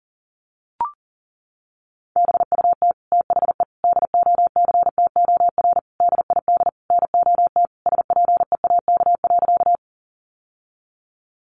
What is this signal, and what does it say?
Morse code, heard as a keyed tone, about 40 wpm, 700 Hz. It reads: BUT THE DOCTOR DID NOT SPEAK.
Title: Sentences from Top 1500 Words 40wpm